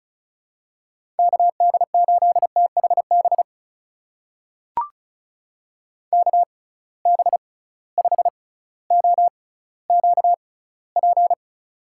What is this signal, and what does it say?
Morse code, heard as a keyed tone, about 36 wpm, spaced as for 18 wpm, 700 Hz. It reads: KD8THB KB5OQP
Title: USA Call Signs - 6 Characters - 6x Character Spacing - 35wpm